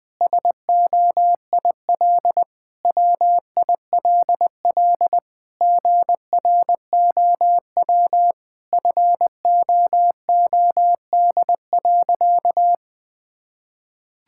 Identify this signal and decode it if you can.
Morse code, heard as a keyed tone, about 20 wpm, 700 Hz. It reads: SOIL WILL GROW FOOD.
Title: Sentences from Top 1000 Words - Review Part 6 - 20wpm